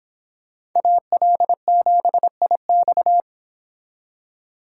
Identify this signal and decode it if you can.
Morse code, heard as a keyed tone, about 26 wpm, 700 Hz. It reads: AL7IX